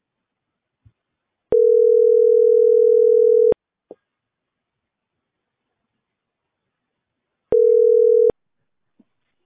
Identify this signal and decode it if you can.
Ringback tone (North America), 440 + 480 Hz.